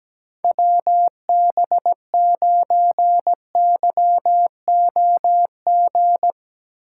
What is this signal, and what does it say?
Morse code, heard as a keyed tone, about 17 wpm, 700 Hz. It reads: WB9YOG
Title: USA Call Signs - 6 Characters - 3x Character Spacing - 17wpm